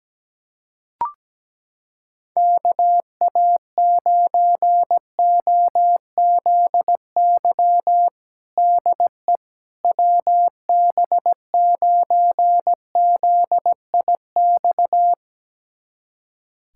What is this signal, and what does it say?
Morse code, heard as a keyed tone, about 17 wpm, 700 Hz. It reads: KA9OZY DE WB9ZIX
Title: USA Callsign DE Callsign - 6 Characters - 17wpm